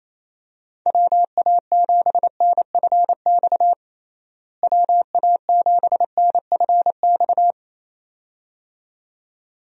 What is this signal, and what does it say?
Morse code, heard as a keyed tone, about 28 wpm, 700 Hz. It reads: WA7NFX WA7NFX
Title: USA Call Signs - 6 Characters - Repeated 2x - 28wpm